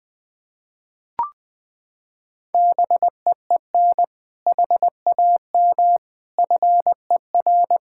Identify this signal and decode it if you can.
Morse code, heard as a keyed tone, about 21 wpm, 700 Hz. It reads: BEEN HAM FER
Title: K1USN SST QSO Practice 20wpm